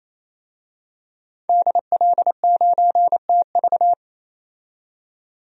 Morse code, heard as a keyed tone, about 28 wpm, 700 Hz. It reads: DL9TV
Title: DXCC-100 Call Signs - 5 Characters - 28wpm